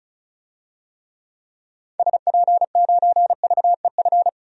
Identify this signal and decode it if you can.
Morse code, heard as a keyed tone, about 36 wpm, 700 Hz. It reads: SP9VEF
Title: DXCC-100 Call Signs - 6 Characters - Repeated 2x - 35wpm